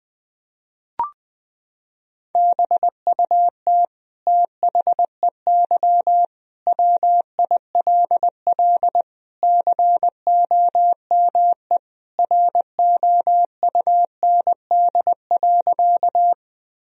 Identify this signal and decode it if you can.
Morse code, heard as a keyed tone, about 20 wpm, 700 Hz. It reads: BUT THEY WILL COME ROUND.
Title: Sentences from Top 1000 Words - Review Part 1 - 20wpm